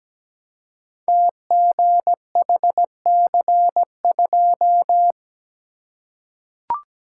Morse code, heard as a keyed tone, about 17 wpm, 700 Hz. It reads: TGHC2
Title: Code Groups - 5 Letter:Numbers - 17wpm